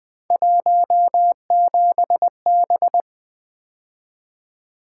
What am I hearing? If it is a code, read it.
Morse code, heard as a keyed tone, about 21 wpm, 700 Hz. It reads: 17B
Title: Three Letter-Number Combinations - 20wpm